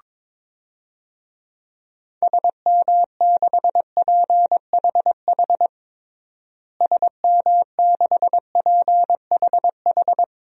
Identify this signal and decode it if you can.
Morse code, heard as a keyed tone, about 22 wpm, 700 Hz. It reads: SM6PHH SM6PHH
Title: DXCC-100 Call Signs - 6 Characters - Repeated 2x - 22wpm